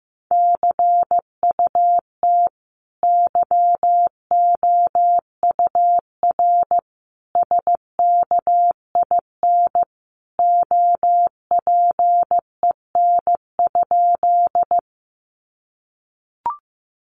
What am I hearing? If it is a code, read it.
Morse code, heard as a keyed tone, about 15 wpm, 700 Hz. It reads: CUT YOUR SKIN OPEN?
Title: Sentences from Top 1300 Words 15wpm